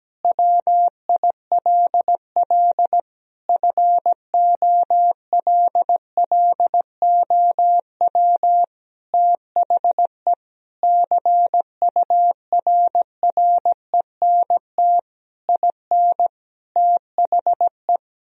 Morse code, heard as a keyed tone, about 17 wpm, 700 Hz. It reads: WILL FOLLOW THE CURRENT IN THE
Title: Sentences from Top 1000 Words - Review Part 4 - 17wpm